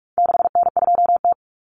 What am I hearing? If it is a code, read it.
Morse code, heard as a keyed tone, about 44 wpm, 700 Hz. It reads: 6N2T
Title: Common Cut Numbers - 4-Digit 45wpm